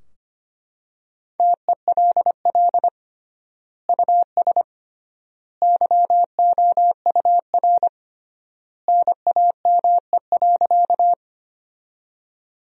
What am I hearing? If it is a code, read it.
Morse code, heard as a keyed tone, about 25 wpm, 700 Hz. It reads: TELL US YOUR NAME.
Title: 3x Word-Spacing - Sentences from Top 200 Words - 25wpm